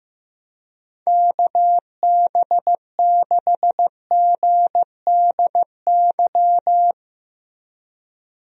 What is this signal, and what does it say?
Morse code, heard as a keyed tone, about 15 wpm, 700 Hz. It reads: KB6GDY
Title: USA Call Signs - 6 Characters - Repeated 3x - 15wpm